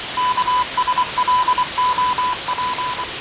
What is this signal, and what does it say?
Morse code, heard as a keyed tone, about 24 wpm, 1000 Hz, with radio noise. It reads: KSLOP
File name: fav22-6824usb-20251029-1420z-voice-test-win10